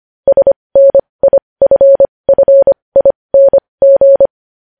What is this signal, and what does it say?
Morse code, heard as a keyed tone, about 25 wpm, spaced as for 20 wpm, 550 Hz. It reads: SNIFFING
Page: News headlines in morse code at 20 wpm